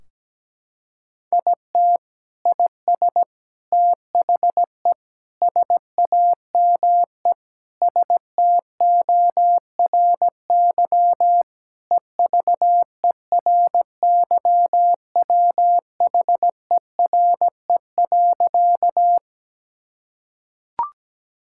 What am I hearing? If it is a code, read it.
Morse code, heard as a keyed tone, about 17 wpm, 700 Hz. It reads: IT IS THE SAME STORY EVERYWHERE.
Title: Sentences from Top 1600 Words 17wpm